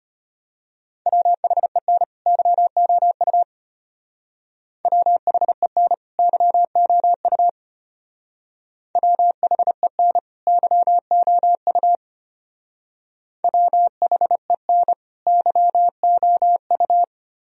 Morse code, heard as a keyed tone, about 30 wpm, 700 Hz. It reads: WHEN YOU WHEN YOU WHEN YOU WHEN YOU
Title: Speed-Racing - Sets of 2 Words 25wpm